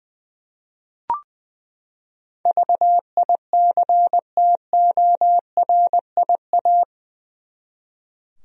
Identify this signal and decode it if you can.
Morse code, heard as a keyed tone, about 21 wpm, 700 Hz. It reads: VICTORIA
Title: USA Common Names - 20wpm